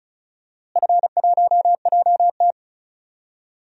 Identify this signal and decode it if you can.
Morse code, heard as a keyed tone, about 36 wpm, 700 Hz. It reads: F1JT
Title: DXCC-100 Call Signs - 4 Characters - Repeated 3x - 35wpm